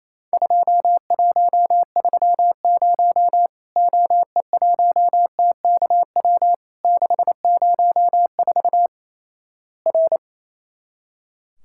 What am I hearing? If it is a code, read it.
Morse code, heard as a keyed tone, about 28 wpm, 700 Hz. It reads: 2130 OE1TKW 604 R
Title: Worked All Europe DX Contest - QTC Exchange - 28wpm